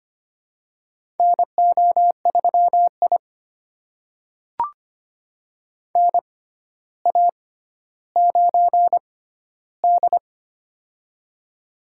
Morse code, heard as a keyed tone, about 25 wpm, spaced as for 13 wpm, 700 Hz. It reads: NO3I NA9D
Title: USA Call Signs - 4 Characters - 6x Character Spacing - 25wpm